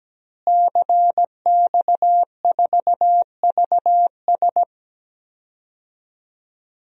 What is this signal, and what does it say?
Morse code, heard as a keyed tone, about 17 wpm, 700 Hz. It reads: CX4VS
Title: DXCC-100 Call Signs - 5 Characters - Repeated 3x - 17wpm